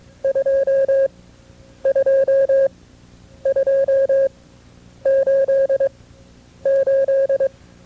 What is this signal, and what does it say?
Morse code, heard as a keyed tone, about 22 wpm, spaced as for 9 wpm, 540 Hz. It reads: 22288
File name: FreeFiles / CW-practice / KB6MT-tape2-side2